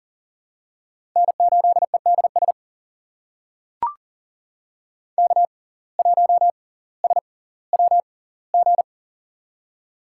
Morse code, heard as a keyed tone, about 40 wpm, spaced as for 20 wpm, 690 Hz. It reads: N8EDS K1SWG
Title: USA Call Signs - 5 Characters - 6x Character Spacing - 40wpm